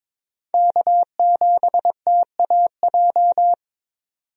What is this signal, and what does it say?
Morse code, heard as a keyed tone, about 22 wpm, 700 Hz. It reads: K7TAJ
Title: USA Call Signs - 5 Characters - 22wpm